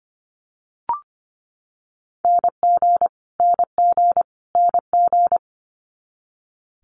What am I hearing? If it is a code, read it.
Morse code, heard as a keyed tone, about 25 wpm, 700 Hz. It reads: NG NG NG
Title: Multi-letter Phonemes Repeated 3x 25wpm